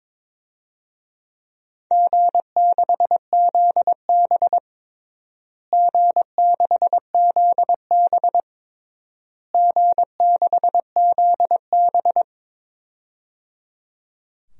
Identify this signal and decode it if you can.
Morse code, heard as a keyed tone, about 22 wpm, 700 Hz. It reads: G6ZB G6ZB G6ZB
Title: DXCC-100 Call Signs - 4 Characters - Repeated 3x - 22wpm